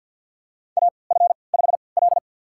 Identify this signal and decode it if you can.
Morse code, heard as a keyed tone, about 52 wpm, spaced as for 28 wpm, 700 Hz. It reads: AF5L
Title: USA Call Signs - 3x Character Spacing - 50wpm